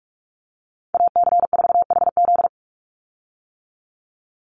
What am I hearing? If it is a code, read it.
Morse code, heard as a keyed tone, about 46 wpm, 690 Hz. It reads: AC4HZ
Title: USA Call Signs 45wpm